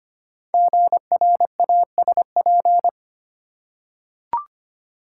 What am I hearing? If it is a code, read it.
Morse code, heard as a keyed tone, about 25 wpm, 700 Hz. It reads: GRASP
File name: 5 Letter Words - First 3 Letters Given - 25wpm